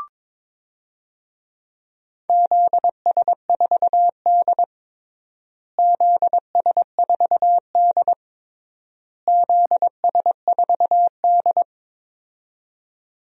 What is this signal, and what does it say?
Morse code, heard as a keyed tone, about 22 wpm, 700 Hz. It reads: ZS4D ZS4D ZS4D
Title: DXCC-100 Call Signs - 4 Characters - Repeated 3x - 22wpm